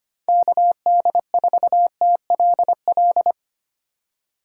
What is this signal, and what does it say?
Morse code, heard as a keyed tone, about 25 wpm, 700 Hz. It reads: KD4TLL
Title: USA Call Signs - 6 Characters - 8x Character Spacing - 25wpm